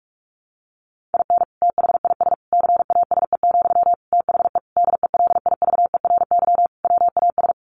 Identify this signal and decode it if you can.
Morse code, heard as a keyed tone, about 44 wpm, 700 Hz. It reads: IN THIS CASE, THE DELIVERY WAS
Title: Sentences from Top 1500 Words 45wpm